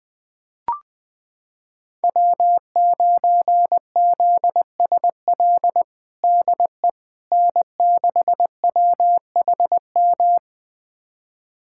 Morse code, heard as a keyed tone, about 21 wpm, 700 Hz. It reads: W9ZSL DE N6WHM
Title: USA Callsign DE Callsign - 5 Characters - 20wpm